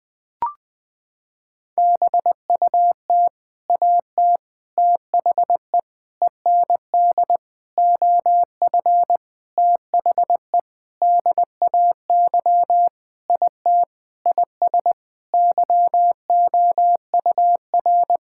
Morse code, heard as a keyed tone, about 20 wpm, 700 Hz. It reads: BUT AT THE END OF THE DAY IT IS YOUR
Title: Sentences with 8-Word Phrases - 20wpm